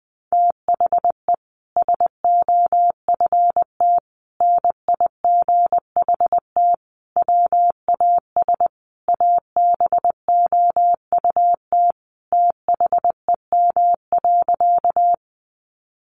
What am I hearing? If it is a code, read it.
Morse code, heard as a keyed tone, about 20 wpm, 700 Hz. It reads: THE SOFT NIGHT WAS ABOUT THEM.
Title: Sentences from Top 1500 Words 20wpm